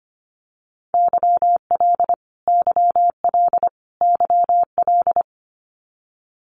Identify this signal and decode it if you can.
Morse code, heard as a keyed tone, about 25 wpm, 700 Hz. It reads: YL YL YL
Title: Abbreviations Repeated 3x - 25wpm